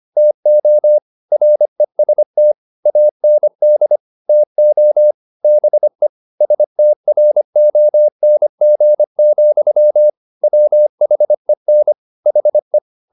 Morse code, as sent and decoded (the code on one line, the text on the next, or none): - --- / .-. ... - / .- -. -.. / - --- / -... / ... - .-. --- -. --. --..-- / .-- .... -. / .... .
TO REST AND TO BE STRONG, WHEN HE